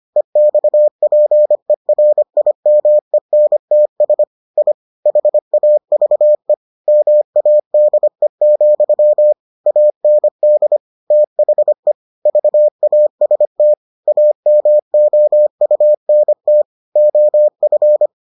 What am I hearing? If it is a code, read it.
Morse code, heard as a keyed tone, about 25 wpm, 600 Hz. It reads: EXPERIMENTS I HAVE MADE, AND THE VAST AMOUNT OF